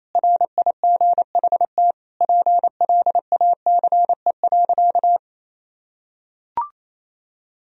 Morse code, heard as a keyed tone, about 28 wpm, 700 Hz. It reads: RIGHT PLACE.
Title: Sentences from Top 1000 Words - Review Part 3 - 28wpm